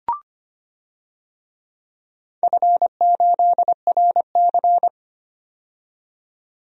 Morse code, heard as a keyed tone, about 25 wpm, 700 Hz. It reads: F8RC
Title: DXCC-100 Call Signs - 4 Characters - 25wpm